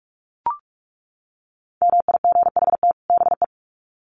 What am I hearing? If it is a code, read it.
Morse code, heard as a keyed tone, about 44 wpm, 700 Hz. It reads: MIGHT BE
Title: Sets of 2 Words 45wpm